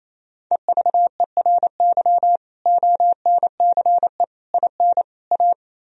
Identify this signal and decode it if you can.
Morse code, heard as a keyed tone, about 37 wpm, 700 Hz. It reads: EVERY ONCE IN A